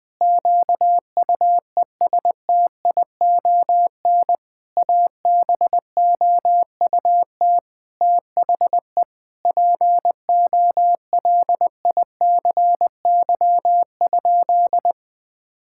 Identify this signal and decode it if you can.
Morse code, heard as a keyed tone, about 21 wpm, 700 Hz. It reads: QUESTION ABOUT THE POLICY?
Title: Sentences with 6-Word Phrases - Spoken Beginning - 20wpm